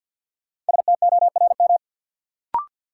Morse code, heard as a keyed tone, about 53 wpm, 700 Hz. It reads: STORM
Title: Sentences from Top 1500 Words - New Word Drill - 50wpm